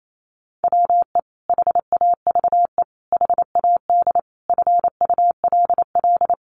Morse code, heard as a keyed tone, about 28 wpm, 700 Hz. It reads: WE HAVE HAD FULL
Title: Sentences from Top 1500 Words 28wpm